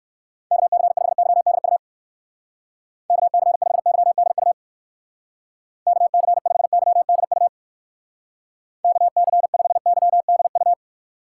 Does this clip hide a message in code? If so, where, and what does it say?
Yes: Morse code from 0.5 to 10.7 s, heard as a keyed tone, about 58 wpm, 700 Hz. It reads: KC5YDU KC5YDU KC5YDU KC5YDU